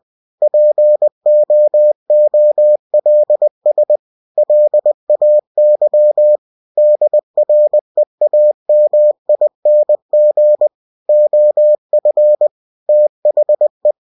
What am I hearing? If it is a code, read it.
Morse code, heard as a keyed tone, about 20 wpm, 600 Hz. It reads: POOLS LAY DREAMING OF THE
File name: cw-books / winnie-the-pooh / winnie-20wpm-page-0101